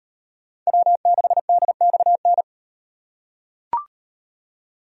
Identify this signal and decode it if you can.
Morse code, heard as a keyed tone, about 38 wpm, 700 Hz. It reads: W6DXN